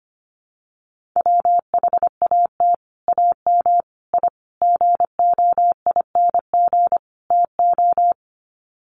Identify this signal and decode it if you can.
Morse code, heard as a keyed tone, about 25 wpm, 700 Hz. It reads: WHAT AM I GOING TO
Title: Sets of 5 Words Encore 25wpm